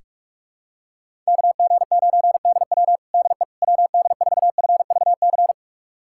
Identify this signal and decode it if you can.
Morse code, heard as a keyed tone, about 46 wpm, 690 Hz. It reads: KG9DW DE WD4FVC
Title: USA Callsign DE Callsign 45wpm